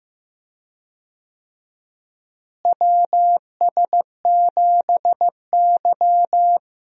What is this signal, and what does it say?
Morse code, heard as a keyed tone, about 15 wpm, 700 Hz. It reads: WS7Y
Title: USA Call Signs - 4 Characters - Repeated 2x - 15wpm